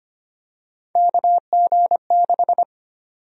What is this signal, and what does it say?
Morse code, heard as a keyed tone, about 25 wpm, 700 Hz. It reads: KG6
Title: USA Call Sign Prefixes - 25wpm